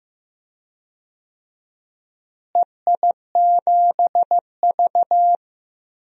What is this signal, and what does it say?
Morse code, heard as a keyed tone, about 15 wpm, 700 Hz. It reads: EI7V
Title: DXCC-100 Call Signs - 4 Characters - Repeated 2x - 15wpm